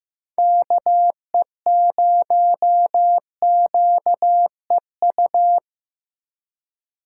Morse code, heard as a keyed tone, about 15 wpm, 700 Hz. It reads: KE0QEU
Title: USA Call Signs - 6 Characters - 4x Character Spacing - 15wpm